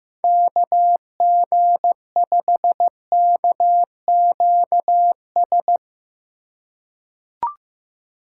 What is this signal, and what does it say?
Morse code, heard as a keyed tone, about 15 wpm, 700 Hz. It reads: KG5KQS
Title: USA Call Signs - 2x Character Spacing - 15wpm